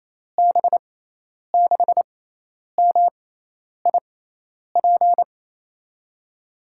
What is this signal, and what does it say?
Morse code, heard as a keyed tone, about 28 wpm, spaced as for 10 wpm, 700 Hz. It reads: B6MIP
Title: USA Call Signs - 6x Character Spacing - 28wpm